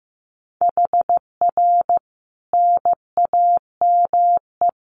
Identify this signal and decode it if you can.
Morse code, heard as a keyed tone, about 14 wpm, 700 Hz. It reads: HR NAME
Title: Common QSO Phrases 15wpm